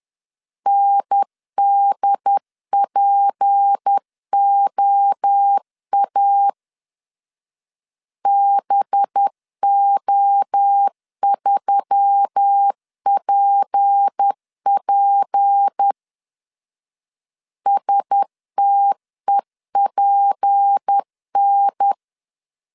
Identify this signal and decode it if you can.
Morse code, heard as a keyed tone, about 10 wpm, 800 Hz. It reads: NDPOA BO3PP STEPN
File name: LEZIONI / 6 PKX0 / 6 P 10wpm 2432